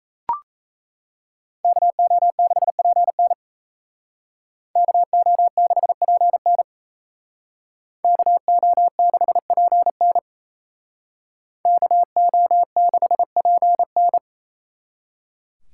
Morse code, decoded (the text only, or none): KO6PN KO6PN KO6PN KO6PN